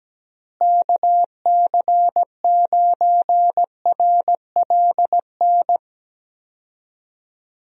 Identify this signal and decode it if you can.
Morse code, heard as a keyed tone, about 17 wpm, 700 Hz. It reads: KC9RLN
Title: USA Call Signs - 6 Characters - Repeated 2x - 17wpm